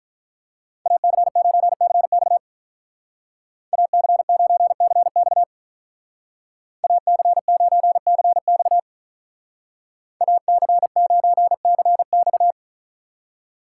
Morse code, heard as a keyed tone, about 43 wpm, 700 Hz. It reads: AC9CX AC9CX AC9CX AC9CX
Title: Speed Racing - USA 2x2 Call Signs 35wpm